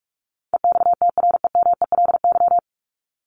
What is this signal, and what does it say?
Morse code, heard as a keyed tone, about 44 wpm, 700 Hz. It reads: EXTREMELY
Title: Top 751-1000 Words 45wpm